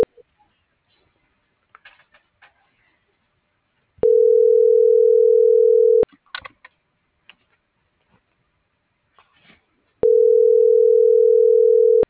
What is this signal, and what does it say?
Ringback tone (North America), 440 + 480 Hz.